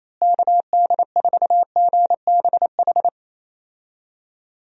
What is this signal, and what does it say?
Morse code, heard as a keyed tone, about 28 wpm, 700 Hz. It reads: KD4GBH
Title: USA Call Signs - 6 Characters - Repeated 2x - 28wpm